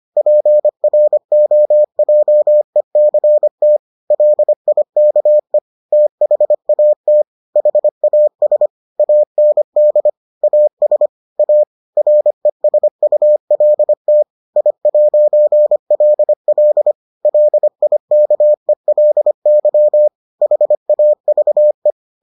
Morse code, heard as a keyed tone, about 25 wpm, 600 Hz. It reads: PROJECT LIKE THAT HAS AND AS A RESULT I'LL LIKELY HAVE